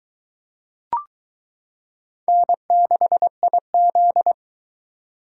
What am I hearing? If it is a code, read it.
Morse code, heard as a keyed tone, about 22 wpm, 700 Hz. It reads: N6IZ